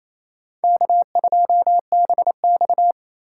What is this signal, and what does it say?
Morse code, heard as a keyed tone, about 34 wpm, 700 Hz. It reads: K2BX
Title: Speed Racing - USA 1x2 Call Signs 28wpm